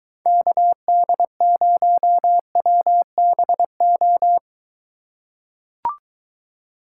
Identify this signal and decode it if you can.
Morse code, heard as a keyed tone, about 22 wpm, 700 Hz. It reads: KD0WBO